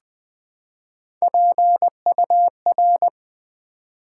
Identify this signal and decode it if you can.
Morse code, heard as a keyed tone, about 20 wpm, 700 Hz. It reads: PUR
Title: Build A Word - 3 Letters At A Time - 6-Letter Words - 20wpm